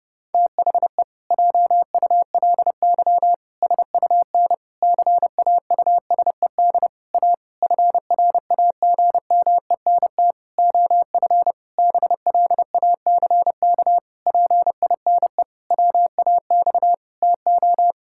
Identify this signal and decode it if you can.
Morse code, heard as a keyed tone, about 30 wpm, 700 Hz. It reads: THE JULY SUN CAUSED A FRAGMENT OF BLACK PINE WAX TO
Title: Pangram Challenge v2 - 30wpm